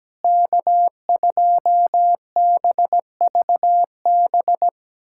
Morse code, heard as a keyed tone, about 17 wpm, 700 Hz. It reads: K2BVB
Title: USA Callsign DE Callsign - 5 Characters - 17wpm